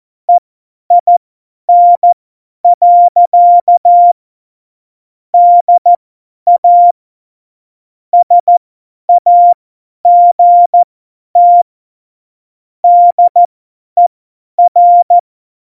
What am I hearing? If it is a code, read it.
Morse code, heard as a keyed tone, about 14 wpm, spaced as for 10 wpm, 700 Hz. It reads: EIN. DA SAGT DER
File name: Morsekurs / Witze / Witz 27. November 2018, 50 F70